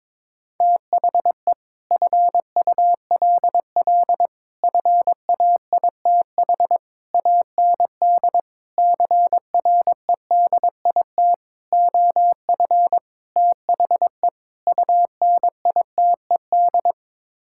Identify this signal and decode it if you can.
Morse code, heard as a keyed tone, about 22 wpm, 700 Hz. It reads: THE FULL FAITH AND CREDIT OF THE UNITED
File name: practice / 22wpm / Sets of 8 Words - 22wpm